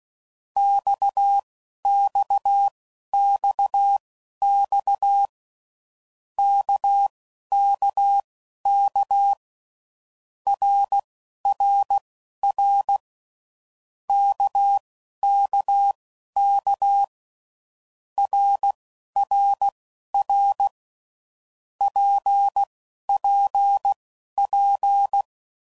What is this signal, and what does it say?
Morse code, heard as a keyed tone, about 16 wpm, spaced as for 11 wpm, 800 Hz. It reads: XXXX KKK RRR KKK RRR PPP
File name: Groups / Basic Letter Groups / RKPX